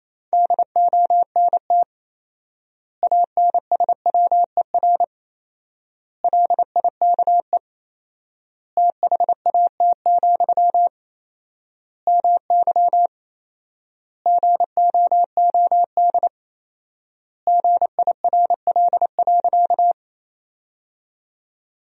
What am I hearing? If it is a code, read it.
Morse code, heard as a keyed tone, about 28 wpm, 700 Hz. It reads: DONT ANSWER LIKE THAT, MY GOOD GIRL.
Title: 4x Word-Spacing - Sentences from Top 400 Words - 28wpm